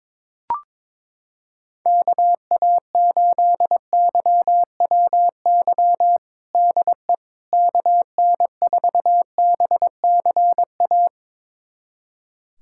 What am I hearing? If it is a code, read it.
Morse code, heard as a keyed tone, about 21 wpm, 700 Hz. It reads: KA8YWY DE KN4BCA
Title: USA Callsign DE Callsign - 6 Characters - 22wpm